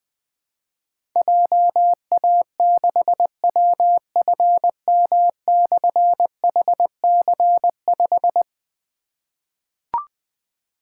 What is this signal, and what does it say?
Morse code, heard as a keyed tone, about 20 wpm, 700 Hz. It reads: JA6WFM/HC5
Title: Difficult Call Signs 20wpm